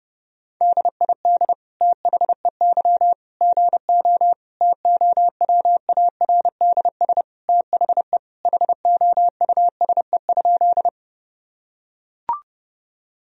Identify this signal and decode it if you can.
Morse code, heard as a keyed tone, about 30 wpm, 700 Hz. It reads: DID THEY GO TOWARDS THE HOUSE?
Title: Sentences from Top 1000 Words - Review Part 2 - 30wpm